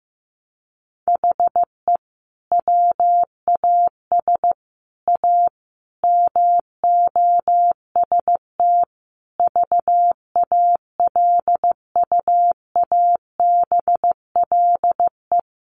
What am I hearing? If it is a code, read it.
Morse code, heard as a keyed tone, about 15 wpm, 700 Hz. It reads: HE WAS A MOST VALUABLE
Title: Sentences from Top 1200 Words 15wpm